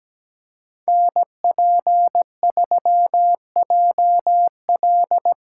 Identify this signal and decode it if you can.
Morse code, heard as a keyed tone, about 20 wpm, 700 Hz. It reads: NP3JL
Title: Speed Racing - USA 2x2 Call Signs 17wpm